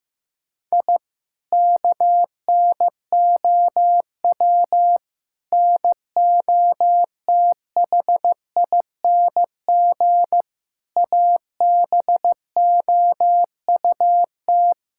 Morse code, heard as a keyed tone, about 16 wpm, 700 Hz. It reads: I KNOW NOTHING ABOUT
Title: Sentences from Top 1000 Words - Review Part 6 - 15wpm